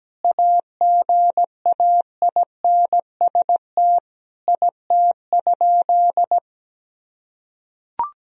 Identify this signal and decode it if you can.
Morse code, heard as a keyed tone, about 17 wpm, 700 Hz. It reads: AGAINST IT?
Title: Sentences from Top 1000 Words - Review Part 1 - 17wpm